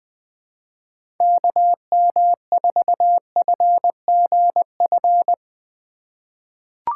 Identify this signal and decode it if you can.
Morse code, heard as a keyed tone, about 20 wpm, 700 Hz. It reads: KM4FGF
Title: USA Call Signs - 6 Characters - 4x Character Spacing - 20wpm